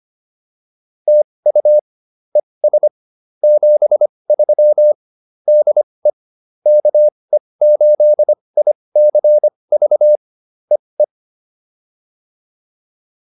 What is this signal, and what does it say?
Morse code, heard as a keyed tone, about 25 wpm, spaced as for 20 wpm, 600 Hz. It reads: TU ES 73 DE KE8ICV EE